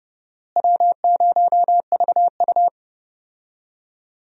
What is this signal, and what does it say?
Morse code, heard as a keyed tone, about 32 wpm, 700 Hz. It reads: W0VU
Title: USA Call Signs - 4 Characters - Repeated 2x - 30wpm